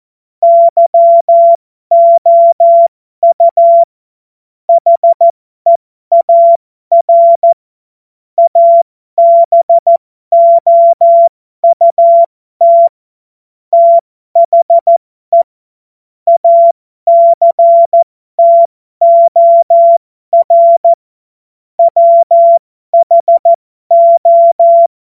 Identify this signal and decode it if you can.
Morse code, heard as a keyed tone, about 14 wpm, spaced as for 12 wpm, 680 Hz. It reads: YOU HEAR ABOUT THE ACTOR WHO
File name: Morsekurs / Witze